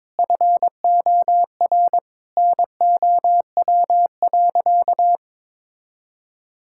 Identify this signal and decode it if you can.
Morse code, heard as a keyed tone, about 22 wpm, 700 Hz. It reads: FOR NOW.
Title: Sentences from Top 1000 Words - Review Part 5 - 22wpm